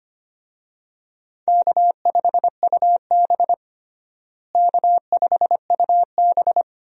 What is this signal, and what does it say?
Morse code, heard as a keyed tone, about 25 wpm, 700 Hz. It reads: K5UB K5UB
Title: USA Call Signs - 4 Characters - Repeated 2x - 25wpm